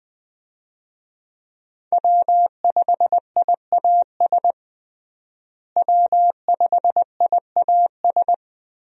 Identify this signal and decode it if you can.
Morse code, heard as a keyed tone, about 20 wpm, 700 Hz. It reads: W5IAS W5IAS
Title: USA Call Signs - 5 Characters - Repeated 3x - 20wpm